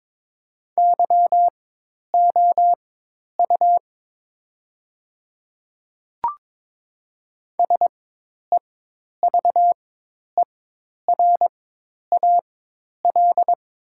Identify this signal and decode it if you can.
Morse code, heard as a keyed tone, about 22 wpm, spaced as for 10 wpm, 700 Hz. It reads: YOU SEVERAL
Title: 4x Character Spacing - Top 500 Words - 22wpm